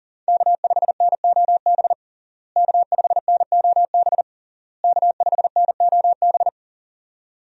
Morse code, heard as a keyed tone, about 39 wpm, 700 Hz. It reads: K5NOB K5NOB K5NOB
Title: USA Call Signs - 5 Characters - Repeated 3x - 40wpm